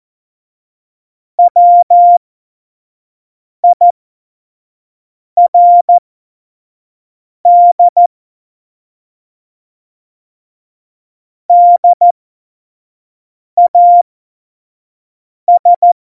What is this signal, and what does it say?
Morse code, heard as a keyed tone, about 14 wpm, spaced as for 5 wpm, 700 Hz. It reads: WIRD DAS
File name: Morsekurs / Witze / Witz 18. September 2018, 25 F70